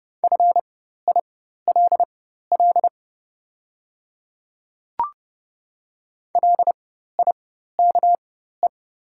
Morse code, heard as a keyed tone, about 30 wpm, spaced as for 14 wpm, 700 Hz. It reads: FILL LIKE
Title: 4x Character Spacing - Top 500 Words - 30wpm